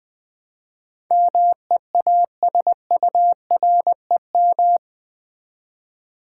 Morse code, heard as a keyed tone, about 20 wpm, 700 Hz. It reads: MEASUREM
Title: Build A Word - 4 Letters At A Time - 11-Letter Words - 20wpm